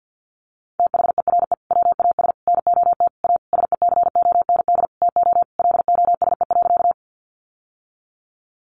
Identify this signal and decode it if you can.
Morse code, heard as a keyed tone, about 50 wpm, 700 Hz. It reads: THERE WAS NOT A SECOND TO LOSE.
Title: Sentences from Top 1100 Words 50wpm